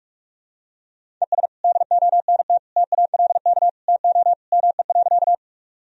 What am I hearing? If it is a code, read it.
Morse code, heard as a keyed tone, about 43 wpm, 700 Hz. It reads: ES DONT TALK TO ME.
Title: Sentences from Top 400 Words and Abbreviations 45wpm